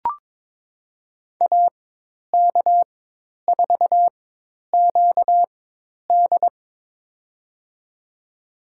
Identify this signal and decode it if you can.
Morse code, heard as a keyed tone, about 21 wpm, spaced as for 10 wpm, 700 Hz. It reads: AK4QD
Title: USA Call Signs - 5 Characters - 4x Character Spacing - 22wpm